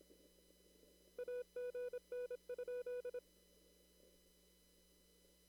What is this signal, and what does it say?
Morse code, heard as a keyed tone, about 26 wpm, 490 Hz. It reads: AGN?
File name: CW / kt5j wpx cw-op-k5pi